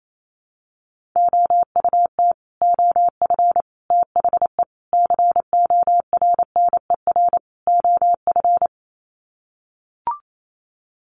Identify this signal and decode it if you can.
Morse code, heard as a keyed tone, about 28 wpm, 700 Hz. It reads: OUT OF THE CORNER OF
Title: Sets of 5 Words 28wpm